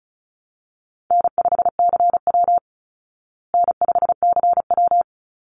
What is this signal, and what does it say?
Morse code, heard as a keyed tone, about 34 wpm, 700 Hz. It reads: N5CW N5CW
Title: Call Signs Repeated 3x - 35wpm